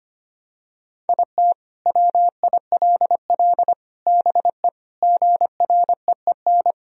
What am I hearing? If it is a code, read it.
Morse code, heard as a keyed tone, about 25 wpm, 700 Hz. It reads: IT WILL BE GREEN
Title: Sentences from Top 1000 Words - Review Part 2 - 25wpm